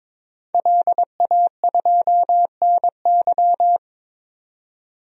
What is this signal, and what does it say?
Morse code, heard as a keyed tone, about 22 wpm, 700 Hz. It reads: LA2NY
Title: DXCC-100 Call Signs - 5 Characters - 22wpm